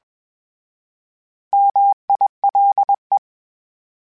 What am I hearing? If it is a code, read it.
Morse code, heard as a keyed tone, about 20 wpm, 800 Hz. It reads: MILE
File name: Misc / Misc-mp3s / Misc @20